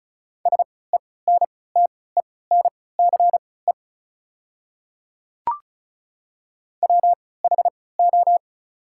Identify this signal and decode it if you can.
Morse code, heard as a keyed tone, about 36 wpm, spaced as for 20 wpm, 700 Hz. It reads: SENTENCE WHO
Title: 3x Character Spacing - Top 200 Words - 35wpm